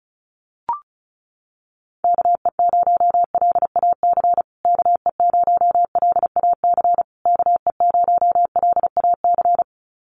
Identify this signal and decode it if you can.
Morse code, heard as a keyed tone, about 34 wpm, 700 Hz. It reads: KE0LAC KE0LAC KE0LAC
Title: USA Call Signs Repeated 3x 35wpm